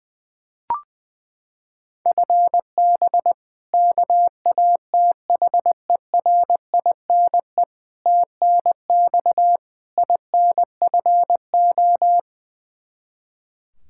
Morse code, heard as a keyed tone, about 20 wpm, 700 Hz. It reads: FB KATHERINE TNX INFO
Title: Practice QSO - Exchange 3 - More Info - 20wpm